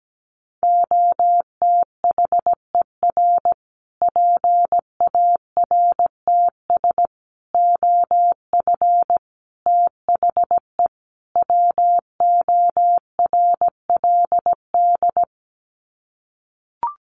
Morse code, heard as a keyed tone, about 17 wpm, 700 Hz. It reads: OTHER PARTS OF THE WORLD
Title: Sets of 5 Words Encore 17wpm